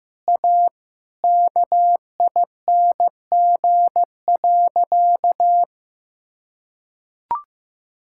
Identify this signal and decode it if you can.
Morse code, heard as a keyed tone, about 15 wpm, 700 Hz. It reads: A KING.
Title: Sentences from Top 1000 Words - Review Part 6 - 15wpm